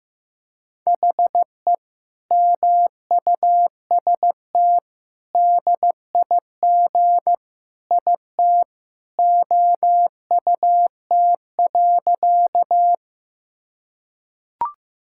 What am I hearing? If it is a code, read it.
Morse code, heard as a keyed tone, about 15 wpm, 700 Hz. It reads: HE MUST DIG IT OUT.
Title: Sentences from Top 1600 Words 15wpm